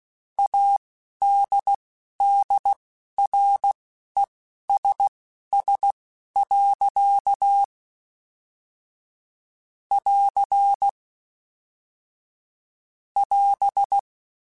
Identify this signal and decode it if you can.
Morse code, heard as a keyed tone, about 16 wpm, spaced as for 11 wpm, 800 Hz. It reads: ADDRESS. <AR> <AS>